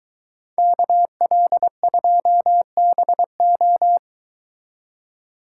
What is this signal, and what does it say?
Morse code, heard as a keyed tone, about 23 wpm, 700 Hz. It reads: KL2BO